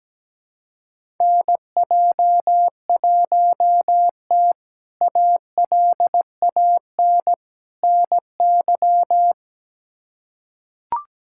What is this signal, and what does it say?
Morse code, heard as a keyed tone, about 17 wpm, 700 Hz. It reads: NJ1T ALAN NY
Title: K1USN SST QSO Practice - 17wpm